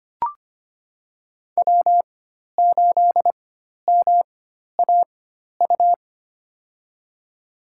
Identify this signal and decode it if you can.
Morse code, heard as a keyed tone, about 25 wpm, spaced as for 12 wpm, 700 Hz. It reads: W8MAU
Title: USA Call Signs - 5 Characters - 4x Character Spacing - 25wpm